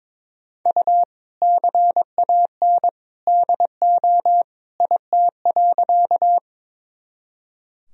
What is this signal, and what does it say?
Morse code, heard as a keyed tone, about 21 wpm, 700 Hz. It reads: U CAN DO IT.
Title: Sentences from Top 100 Words and Abbreviations 22wpm